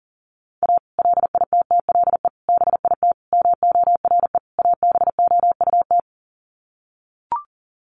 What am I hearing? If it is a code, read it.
Morse code, heard as a keyed tone, about 40 wpm, 700 Hz. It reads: A LITTLE BIT MORE ABOUT
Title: Sets of 5 Words Encore2 40wpm